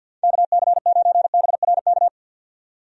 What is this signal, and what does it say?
Morse code, heard as a keyed tone, about 53 wpm, 700 Hz. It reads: KC9BRK
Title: USA Call Signs - 6 Characters - 50wpm